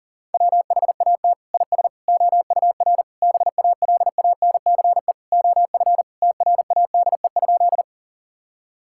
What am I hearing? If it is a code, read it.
Morse code, heard as a keyed tone, about 40 wpm, 700 Hz. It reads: WHAT IS OUR BALANCE OF TRADE?
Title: Sentences from Top 1000 Words - Review Part 6 - 40wpm